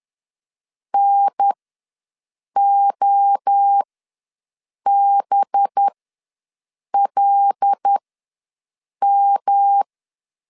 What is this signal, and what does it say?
Morse code, heard as a keyed tone, about 10 wpm, spaced as for 6 wpm, 800 Hz. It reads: NOBLM
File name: LEZIONI / 5 LBVC / 5 LB ws 10wpm 3616